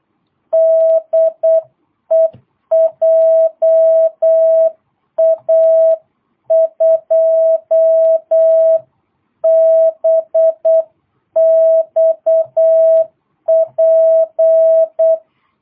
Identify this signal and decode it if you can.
Morse code, heard as a keyed tone, about 8 wpm, 660 Hz. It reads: DE JA2BXP
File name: jh8cbh / 2022cw